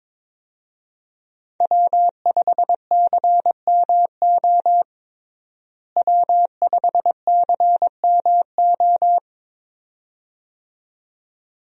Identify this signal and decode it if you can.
Morse code, heard as a keyed tone, about 21 wpm, 700 Hz. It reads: W5CMO W5CMO
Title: USA Call Signs - 5 Characters - Repeated 2x - 22wpm